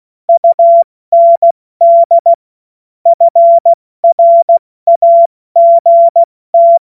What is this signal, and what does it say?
Morse code, heard as a keyed tone, about 16 wpm, spaced as for 14 wpm, 680 Hz. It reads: UND FRAGT
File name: Morsekurs / Witze / Witz 25. April 70 F80